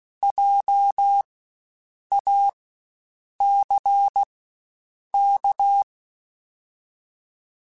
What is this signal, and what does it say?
Morse code, heard as a keyed tone, about 16 wpm, spaced as for 7 wpm, 800 Hz. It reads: JACK
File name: Spaced Alphas / Spaced Alphas mp3s / Js sp @15